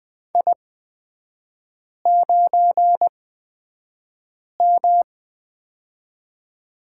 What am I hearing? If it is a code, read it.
Morse code, heard as a keyed tone, about 20 wpm, spaced as for 5 wpm, 700 Hz. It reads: I9M
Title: 229 Lesson 37e - Letter x - Three Characters 20wpm